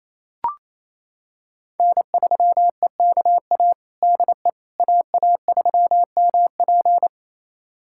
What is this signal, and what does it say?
Morse code, heard as a keyed tone, about 28 wpm, 700 Hz. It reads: N3EKA DE AA3MP
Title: USA Callsign DE Callsign - 5 Characters - 28wpm